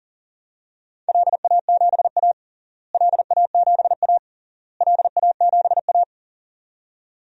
Morse code, heard as a keyed tone, about 40 wpm, 700 Hz. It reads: LA7A LA7A LA7A
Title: DXCC-100 Call Signs - 4 Characters - Repeated 3x - 40wpm